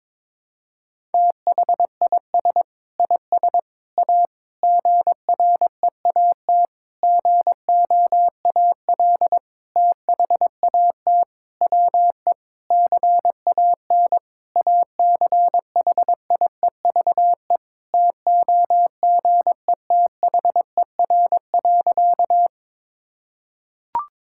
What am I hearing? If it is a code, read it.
Morse code, heard as a keyed tone, about 22 wpm, 700 Hz. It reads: THIS IS A GREAT GOAL THAT WE CAN ACHIEVE TOGETHER.
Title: Sentences from Top 1000 Words - Review Part 6 - 22wpm